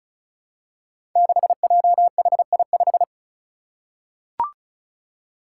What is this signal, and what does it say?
Morse code, heard as a keyed tone, about 34 wpm, 700 Hz. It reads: 6JHI5
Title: Five Challenging Letter-Number Combinations 35wpm